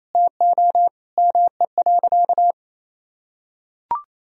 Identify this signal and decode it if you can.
Morse code, heard as a keyed tone, about 28 wpm, 700 Hz. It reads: TO ME.
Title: Sentences from Top 200 Words and Abbreviations 28wpm